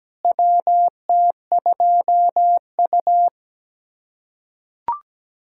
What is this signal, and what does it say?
Morse code, heard as a keyed tone, about 17 wpm, 700 Hz. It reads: WT2U
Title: USA Callsign DE Callsign - 4 Characters - 17wpm